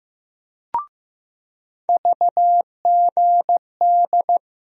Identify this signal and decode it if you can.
Morse code, heard as a keyed tone, about 16 wpm, 700 Hz. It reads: VGD
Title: USA Call Sign Postfixes - 3 Letters 15wpm